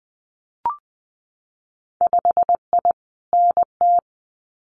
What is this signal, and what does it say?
Morse code, heard as a keyed tone, about 20 wpm, 700 Hz. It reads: 5I NT
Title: Winter Field Day Exchanges Encore 20wpm